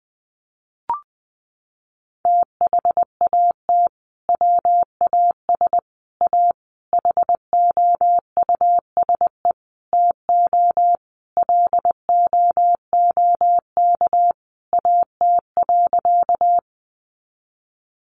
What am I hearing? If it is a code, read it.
Morse code, heard as a keyed tone, about 19 wpm, 700 Hz. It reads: THAT WAS A HOUSE TO LOOK AT.
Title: Sentences from Top 200 Words 20wpm